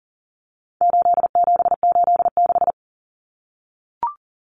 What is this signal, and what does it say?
Morse code, heard as a keyed tone, about 40 wpm, 700 Hz. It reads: 8786
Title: Numbers - 4-Digit 40wpm